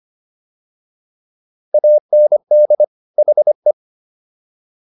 Morse code, heard as a keyed tone, about 25 wpm, 600 Hz. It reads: AND HE
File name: WD104 25